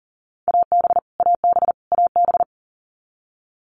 Morse code, heard as a keyed tone, about 40 wpm, 700 Hz. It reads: AB AB AB
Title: Canadian Provinces and Territories - Repeated 3x - 40wpm